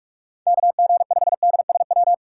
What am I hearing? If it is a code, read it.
Morse code, heard as a keyed tone, about 45 wpm, 700 Hz. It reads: KG5DSW
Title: USA Call Signs - 6 Characters - 6x Character Spacing - 45wpm